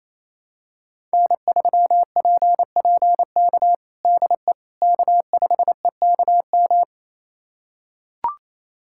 Morse code, heard as a keyed tone, about 28 wpm, 700 Hz. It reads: N3PPK DE K5EKM
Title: USA Callsign DE Callsign - 5 Characters - 28wpm